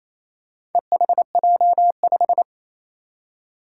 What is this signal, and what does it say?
Morse code, heard as a keyed tone, about 27 wpm, 700 Hz. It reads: EHJ5
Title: Four Challenging Letter-Number Combinations 28wpm